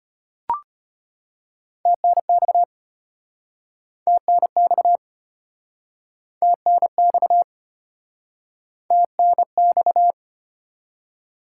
Morse code, heard as a keyed tone, about 36 wpm, 690 Hz. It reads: TNX TNX TNX TNX